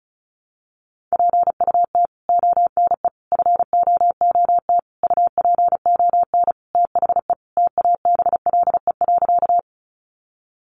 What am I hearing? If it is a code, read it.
Morse code, heard as a keyed tone, about 34 wpm, 700 Hz. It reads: PUT ONE FOOT UPON THE TABLE.
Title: Sentences from Top 1100 Words 35wpm